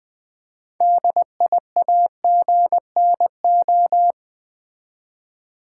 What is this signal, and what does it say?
Morse code, heard as a keyed tone, about 20 wpm, 690 Hz. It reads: DIAGNO
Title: Build A Word - 2 Letters At A Time - 8-Letter Words - 20wpm